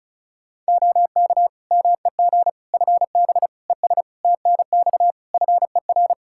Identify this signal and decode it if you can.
Morse code, heard as a keyed tone, about 36 wpm, 700 Hz. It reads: OK MEG FB ES TNX FER
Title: Practice QSO - Exchange 2 - Info - 35wpm